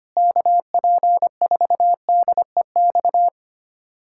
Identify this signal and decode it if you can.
Morse code, heard as a keyed tone, about 25 wpm, 700 Hz. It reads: KP4DEX
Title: USA Callsign DE Callsign - 6 Characters - 25wpm